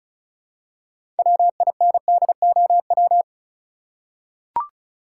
Morse code, heard as a keyed tone, about 34 wpm, 700 Hz. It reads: WINDOW
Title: Six Letter Words - 35wpm